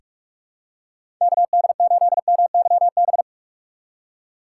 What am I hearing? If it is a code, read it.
Morse code, heard as a keyed tone, about 45 wpm, 700 Hz. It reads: KD8MYB
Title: USA Call Signs - 6 Characters - 4x Character Spacing - 45wpm